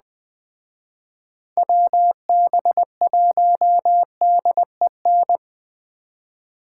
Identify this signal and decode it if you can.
Morse code, heard as a keyed tone, about 20 wpm, 700 Hz. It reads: WB1DEN
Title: Speed Racing - USA Call Signs 17wpm